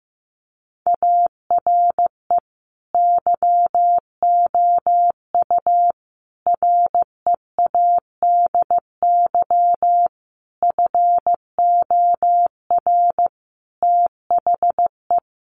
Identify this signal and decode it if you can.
Morse code, heard as a keyed tone, about 15 wpm, 700 Hz. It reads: ARE YOU READY FOR THE
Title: Sentences from Top 1300 Words 15wpm